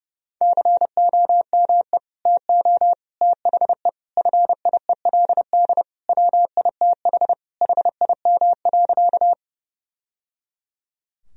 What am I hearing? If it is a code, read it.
Morse code, heard as a keyed tone, about 30 wpm, 700 Hz. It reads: COME TO THE FIELD WITH HIM.
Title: Sentences from Top 1000 Words - Review Part 2 - 30wpm